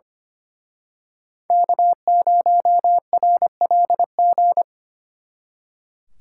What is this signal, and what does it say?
Morse code, heard as a keyed tone, about 29 wpm, 700 Hz. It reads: K0RLG
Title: Speed Racing - USA 1x3 Call Signs 25wpm